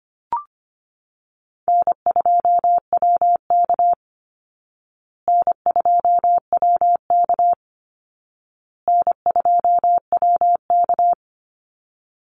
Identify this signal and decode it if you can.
Morse code, heard as a keyed tone, about 25 wpm, 700 Hz. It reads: N2WK N2WK N2WK
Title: Call Signs Repeated 2x - 25wpm